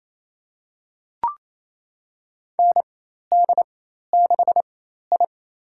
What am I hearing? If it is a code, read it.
Morse code, heard as a keyed tone, about 28 wpm, spaced as for 13 wpm, 700 Hz. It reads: ND6I
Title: USA Call Signs - 4 Characters - 4x Character Spacing - 28wpm